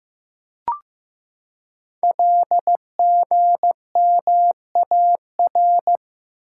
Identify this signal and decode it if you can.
Morse code, heard as a keyed tone, about 16 wpm, 700 Hz. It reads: LGMAR
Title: Code Groups - 5 Letters - 15wpm